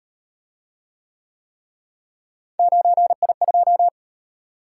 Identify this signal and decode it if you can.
Morse code, heard as a keyed tone, about 38 wpm, 700 Hz. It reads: 9I2